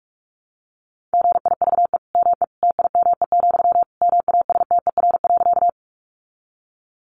Morse code, heard as a keyed tone, about 44 wpm, 700 Hz. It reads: GIVE ME TIME, MASTER.
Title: Sentences from Top 1300 Words 45wpm